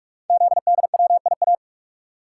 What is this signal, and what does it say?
Morse code, heard as a keyed tone, about 44 wpm, 700 Hz. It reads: ZDWIA
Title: Code Groups - 5 Letters - 45wpm